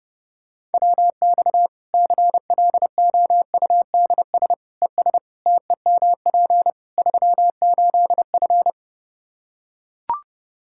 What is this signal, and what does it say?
Morse code, heard as a keyed tone, about 30 wpm, 700 Hz. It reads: WX CLOUDS ES TEMP 38F
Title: Practice QSO - Exchange 2 - Info - 30wpm